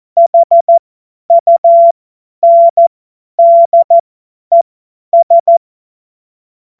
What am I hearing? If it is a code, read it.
Morse code, heard as a keyed tone, about 14 wpm, spaced as for 10 wpm, 680 Hz. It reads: HUNDES
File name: Morsekurs / Witze / Witz 02 April 2024 50 F70